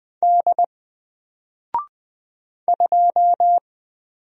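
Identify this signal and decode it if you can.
Morse code, heard as a keyed tone, about 20 wpm, 700 Hz. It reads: D 2
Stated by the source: Single letter and number Morse drill at 20 wpm, from N4LCD.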